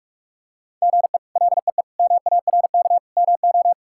Morse code, heard as a keyed tone, about 46 wpm, 700 Hz. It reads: GE LEE MARK MO
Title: K1USN SST QSO Practice - 45wpm